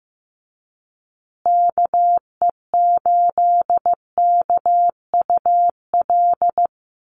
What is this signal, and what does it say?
Morse code, heard as a keyed tone, about 15 wpm, 700 Hz. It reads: KE8KUL
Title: USA Call Signs Repeated 3x 15wpm